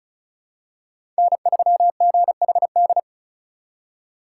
Morse code, heard as a keyed tone, about 36 wpm, 700 Hz. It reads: N3GHD
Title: USA Call Signs - 5 Characters - 2x Character Spacing - 35wpm